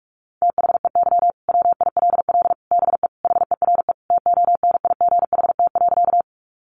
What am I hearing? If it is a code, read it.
Morse code, heard as a keyed tone, about 44 wpm, 700 Hz. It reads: THEY WILL BE HERE TONIGHT.
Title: Sentences from Top 600 Words 45wpm